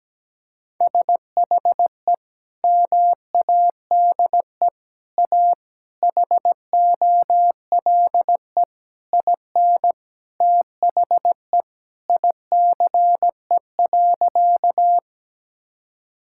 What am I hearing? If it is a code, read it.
Morse code, heard as a keyed tone, about 17 wpm, 700 Hz. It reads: SHE MADE A HOLE IN THE ICE.
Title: Sentences from Top 1600 Words 17wpm